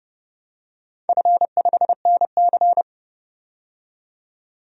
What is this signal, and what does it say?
Morse code, heard as a keyed tone, about 30 wpm, 700 Hz. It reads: F5NC
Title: DXCC-100 Call Signs - 4 Characters - 30wpm